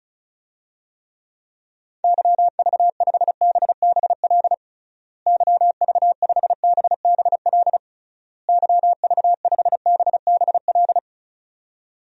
Morse code, heard as a keyed tone, about 34 wpm, 700 Hz. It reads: YV5BBL YV5BBL YV5BBL
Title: DXCC-100 Call Signs - 6 Characters - Repeated 3x - 35wpm